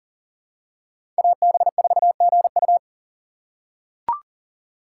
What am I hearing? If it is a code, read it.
Morse code, heard as a keyed tone, about 40 wpm, 700 Hz. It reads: AB4GU
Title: USA Call Signs - 5 Characters - 6x Character Spacing - 40wpm